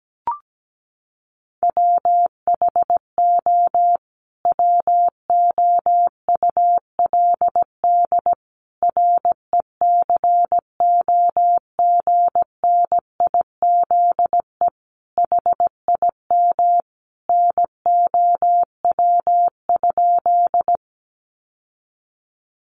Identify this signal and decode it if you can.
Morse code, heard as a keyed tone, about 17 wpm, 700 Hz. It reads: WHO WOULD RECOGNIZE HIM NOW?
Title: Sentences from Top 1400 Words 17wpm